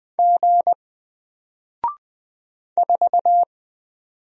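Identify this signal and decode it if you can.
Morse code, heard as a keyed tone, about 20 wpm, 700 Hz. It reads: G 4
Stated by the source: Single letter and number Morse drill at 20 wpm, from N4LCD.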